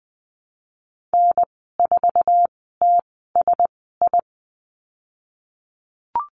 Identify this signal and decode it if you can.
Morse code, heard as a keyed tone, about 19 wpm, spaced as for 14 wpm, 700 Hz. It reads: N4TSI
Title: 048 Lesson 9g - Number 4 - Call Signs 20wpm